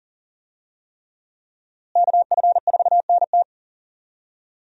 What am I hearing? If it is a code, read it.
Morse code, heard as a keyed tone, about 37 wpm, 700 Hz. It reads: KF4NT
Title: USA Call Signs - 5 Characters - Repeated 3x - 40wpm